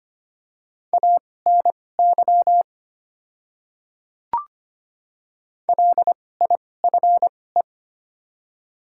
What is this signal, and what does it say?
Morse code, heard as a keyed tone, about 25 wpm, spaced as for 18 wpm, 700 Hz. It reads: ANY LIFE
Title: 2x Character Spacing - Top 500 Words - 25wpm